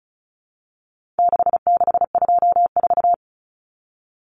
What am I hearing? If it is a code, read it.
Morse code, heard as a keyed tone, about 34 wpm, 700 Hz. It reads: 6624
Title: Numbers - 4-Digit 35wpm